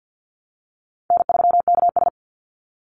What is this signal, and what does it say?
Morse code, heard as a keyed tone, about 50 wpm, 700 Hz. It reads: N3KS
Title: Call Signs Repeated 3x - 50wpm